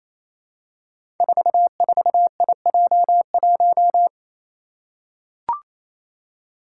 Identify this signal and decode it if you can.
Morse code, heard as a keyed tone, about 27 wpm, 700 Hz. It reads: 44IJ1
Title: Five Challenging Letter-Number Combinations 28wpm